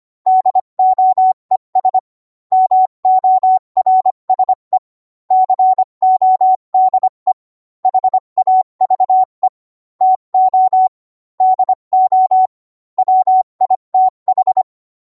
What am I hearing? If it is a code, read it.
Morse code, heard as a keyed tone, about 25 wpm, 750 Hz. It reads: DOES MORSE CODE HAVE TO DO WITH